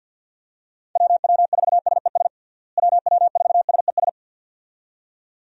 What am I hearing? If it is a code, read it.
Morse code, heard as a keyed tone, about 48 wpm, 700 Hz. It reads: WW4SES WW4SES
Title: USA Call Signs - 6 Characters - Repeated 2x - 50wpm